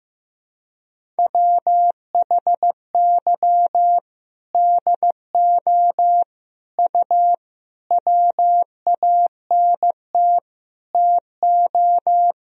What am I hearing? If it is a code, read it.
Morse code, heard as a keyed tone, about 15 wpm, 700 Hz. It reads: WHY DO U WANT TO
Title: Sentences from Top 200 Words and Abbreviations 15wpm